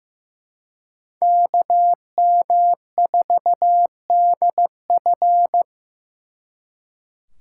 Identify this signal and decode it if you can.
Morse code, heard as a keyed tone, about 18 wpm, 700 Hz. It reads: KM4DF
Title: Speed Racing - USA 2x2 Call Signs 15wpm